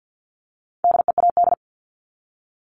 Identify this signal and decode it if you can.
Morse code, heard as a keyed tone, about 50 wpm, 700 Hz. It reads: DEAD
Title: Top 1251-1500 Words 50wpm